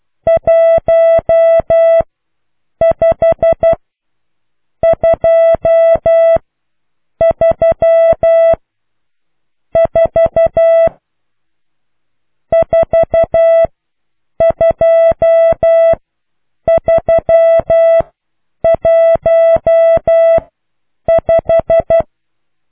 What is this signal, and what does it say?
Morse code, heard as a keyed tone, about 12 wpm, 660 Hz. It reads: 1 5 2 3 4 4 2 3 1 5